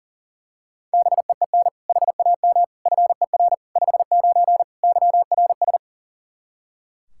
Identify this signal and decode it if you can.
Morse code, heard as a keyed tone, about 40 wpm, 700 Hz. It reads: BEEN HAM FER 59 YRS
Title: Practice QSO - Exchange 3 - More Info - 40wpm